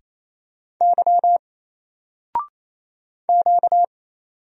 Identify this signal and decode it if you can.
Morse code, heard as a keyed tone, about 27 wpm, 700 Hz. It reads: Y Q
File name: Single-Letter-Number-28wpm